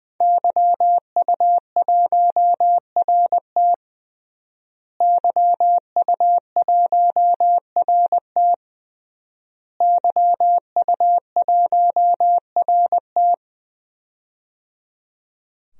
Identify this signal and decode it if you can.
Morse code, heard as a keyed tone, about 20 wpm, 700 Hz. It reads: YU1RT YU1RT YU1RT
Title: DXCC-100 Call Signs - 5 Characters - Repeated 3x - 20wpm